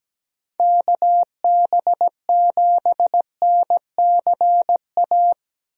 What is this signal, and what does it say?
Morse code, heard as a keyed tone, about 17 wpm, 700 Hz. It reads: KB7NCA
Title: USA Call Signs - 6 Characters - Repeated 2x - 17wpm